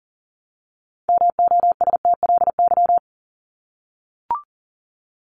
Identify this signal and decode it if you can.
Morse code, heard as a keyed tone, about 40 wpm, 700 Hz. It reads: MOSTLY
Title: Top 1001-1250 Words 40wpm